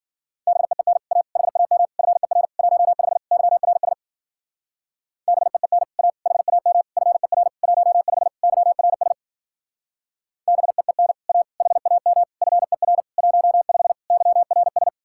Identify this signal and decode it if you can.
Morse code, heard as a keyed tone, about 47 wpm, 700 Hz. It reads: BEEN A HAM FER 15 YRS BEEN A HAM FER 15 YRS BEEN A HAM FER 15 YRS
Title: Speed-Racing Common QSO Phrases Encore - 40wpm